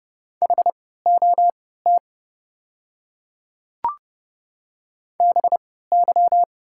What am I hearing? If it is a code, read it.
Morse code, heard as a keyed tone, about 30 wpm, spaced as for 17 wpm, 690 Hz. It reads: HOT BY
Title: 3x Character Spacing - Top 200 Words - 30wpm